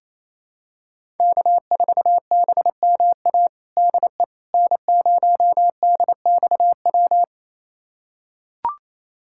Morse code, heard as a keyed tone, about 28 wpm, 700 Hz. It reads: K4BMA DE N0DXW
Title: USA Callsign DE Callsign - 5 Characters - 28wpm